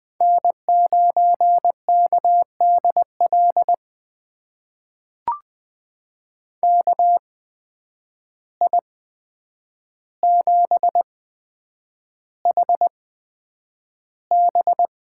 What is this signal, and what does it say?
Morse code, heard as a keyed tone, about 20 wpm, spaced as for 9 wpm, 700 Hz. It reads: N9KDL KI7HB
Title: USA Call Signs - 8x Character Spacing - 20wpm